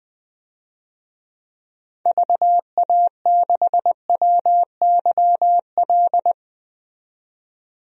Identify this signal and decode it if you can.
Morse code, heard as a keyed tone, about 20 wpm, 700 Hz. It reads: VA6WYL
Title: DXCC-100 Call Signs - 6 Characters - 20wpm